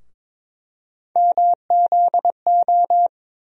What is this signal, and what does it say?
Morse code, heard as a keyed tone, about 22 wpm, 700 Hz. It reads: MZO
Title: USA Call Sign Postfixes - 3 Letters 22wpm